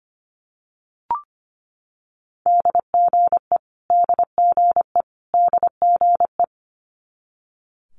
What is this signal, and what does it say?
Morse code, heard as a keyed tone, about 25 wpm, 700 Hz. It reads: DGE DGE DGE
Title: Multi-letter Phonemes Repeated 3x 25wpm